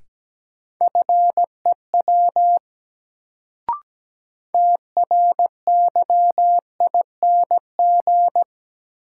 Morse code, heard as a keyed tone, about 17 wpm, 700 Hz. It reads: FEW TRYING
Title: Sets of 2 Words - New Word Drill - 17wpm